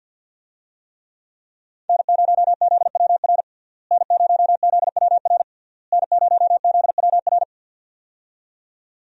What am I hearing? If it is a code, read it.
Morse code, heard as a keyed tone, about 51 wpm, 700 Hz. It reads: N0ZWR N0ZWR N0ZWR
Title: USA Call Signs - 5 Characters - Repeated 3x - 50wpm